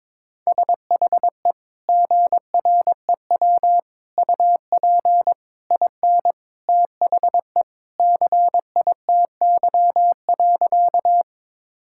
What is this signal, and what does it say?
Morse code, heard as a keyed tone, about 21 wpm, 700 Hz. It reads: SHE GREW UP IN THE CITY.
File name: practice / 22wpm / Sentences with 2-Word Phrases - 22wpm